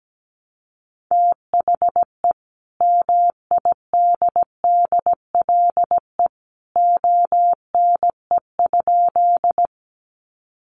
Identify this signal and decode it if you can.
Morse code, heard as a keyed tone, about 17 wpm, 700 Hz. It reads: THE MIDDLE ONE?
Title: Sentences from Top 900 Words 17wpm